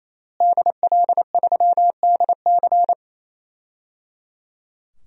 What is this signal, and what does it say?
Morse code, heard as a keyed tone, about 28 wpm, 700 Hz. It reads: DL3DC
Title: DXCC-100 Call Signs - 5 Characters - Repeated 3x - 28wpm